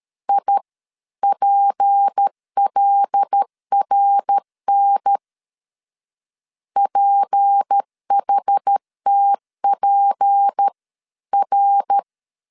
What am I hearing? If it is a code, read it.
Morse code, heard as a keyed tone, about 13 wpm, spaced as for 11 wpm, 800 Hz. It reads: IPLRN PHTPR
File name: LEZIONI / 6 PKX0 / 6 P 12wpm 2033